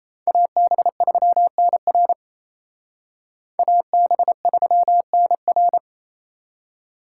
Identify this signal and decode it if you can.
Morse code, heard as a keyed tone, about 36 wpm, 700 Hz. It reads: AB3NR AB3NR